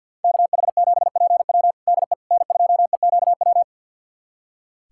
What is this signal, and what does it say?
Morse code, heard as a keyed tone, about 47 wpm, 700 Hz. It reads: KH7PW DE N2EQW
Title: USA Callsign DE Callsign - 5 Characters - 50wpm